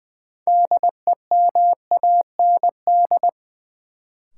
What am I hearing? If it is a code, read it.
Morse code, heard as a keyed tone, about 20 wpm, 700 Hz. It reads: DEMAND
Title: Sentences from Top 800 Words - New Word Drill - 20wpm